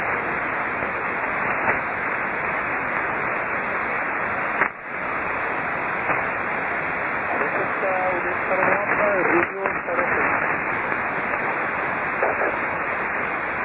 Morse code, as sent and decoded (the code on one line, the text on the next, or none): none